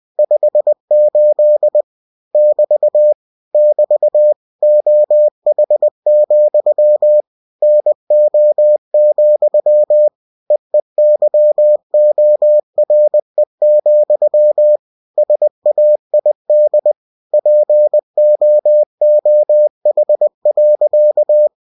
Morse code, heard as a keyed tone, about 20 wpm, 600 Hz. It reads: 58 <BT> <BT> OH, NO, EEYORE, SAID POOH.
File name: cw-books / winnie-the-pooh / winnie-20wpm-page-0058